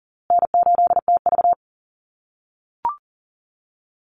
Morse code, heard as a keyed tone, about 40 wpm, 700 Hz. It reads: N8TV
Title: USA Call Signs Repeated 3x 40wpm